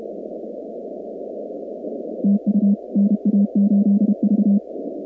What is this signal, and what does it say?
Morse code, heard as a keyed tone, about 32 wpm, 200 Hz. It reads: TU NA8V
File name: contests / 2019-cw-wpx-kt5j-k5pi- -2330 0-46